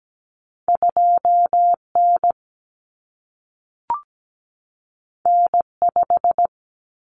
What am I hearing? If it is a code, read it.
Morse code, heard as a keyed tone, about 17 wpm, 700 Hz. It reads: 2N N5
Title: Common Cut Numbers - 2-Digit 17wpm